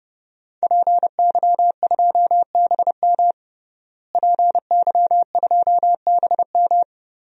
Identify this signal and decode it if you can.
Morse code, heard as a keyed tone, about 30 wpm, 700 Hz. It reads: PY2BM PY2BM
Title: DXCC-100 Call Signs - 5 Characters - Repeated 3x - 30wpm